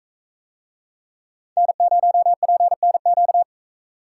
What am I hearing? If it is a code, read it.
Morse code, heard as a keyed tone, about 40 wpm, 700 Hz. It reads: N0PNQ